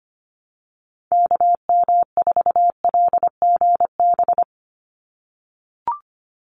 Morse code, heard as a keyed tone, about 25 wpm, 690 Hz. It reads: KM4LGB
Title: USA Call Signs 25wpm